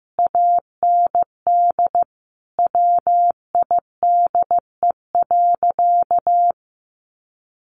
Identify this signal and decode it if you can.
Morse code, heard as a keyed tone, about 14 wpm, 700 Hz. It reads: AND WIDE.
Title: Sentences from Top 1200 Words 15wpm